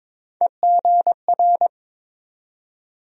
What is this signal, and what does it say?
Morse code, heard as a keyed tone, about 22 wpm, 700 Hz. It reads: EGR
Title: USA Call Sign Postfixes - 3 Letters 22wpm